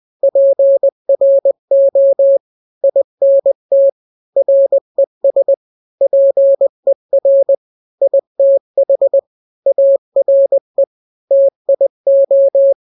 Morse code, heard as a keyed tone, about 20 wpm, 550 Hz. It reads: PRO INT RES PER ITH ARE TIO
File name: n-grams / 3-grams 20 18 WPM